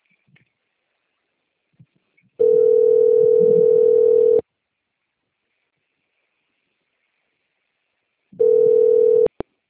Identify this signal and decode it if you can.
Ringback tone (North America), 440 + 480 Hz.